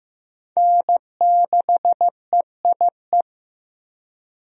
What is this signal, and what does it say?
Morse code, heard as a keyed tone, about 15 wpm, 700 Hz. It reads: N6EIE
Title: USA Call Signs - 5 Characters - Repeated 3x - 15wpm